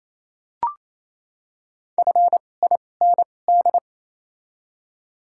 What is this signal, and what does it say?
Morse code, heard as a keyed tone, about 28 wpm, spaced as for 20 wpm, 700 Hz. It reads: FIND
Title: 2x Character Spacing - Top 100 Words - 28wpm